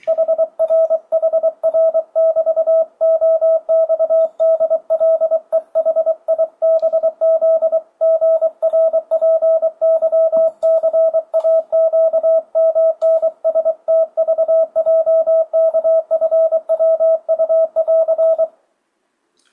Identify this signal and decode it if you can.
Morse code, heard as a keyed tone, about 23 wpm, 650 Hz. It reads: HRHR<BT>OXDLEHIBZGRPYCAQMNSTVJKFWU<AR>
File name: 2025cw / speedup / 24wpm